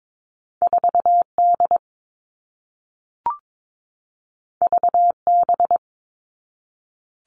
Morse code, heard as a keyed tone, about 22 wpm, 700 Hz. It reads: 4D VB
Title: All Cut Numbers - 2-Digit 22wpm